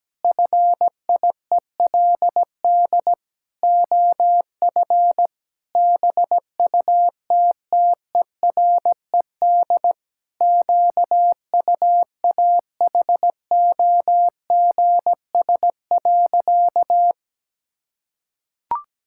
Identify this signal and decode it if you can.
Morse code, heard as a keyed tone, about 17 wpm, 700 Hz. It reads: FIELD OF BUTTERED QUAHOGS.
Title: Pangram Challenge v2 - 17wpm